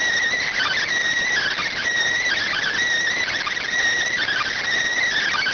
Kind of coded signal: Emergency Alert System (SAME) tones

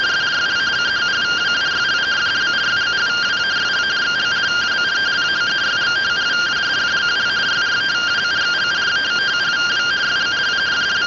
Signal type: two-tone teleprinter signal

